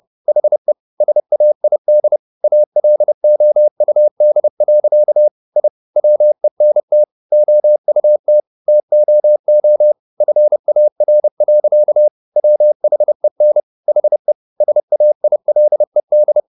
Morse code, heard as a keyed tone, about 30 wpm, 600 Hz. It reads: HE SAID ALOUD. I WENT OUT TOO FAR. WHEN HE SAILED